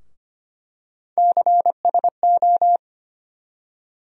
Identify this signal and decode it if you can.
Morse code, heard as a keyed tone, about 25 wpm, 700 Hz. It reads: CSO
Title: USA Call Sign Postfixes - 3 Letters 25wpm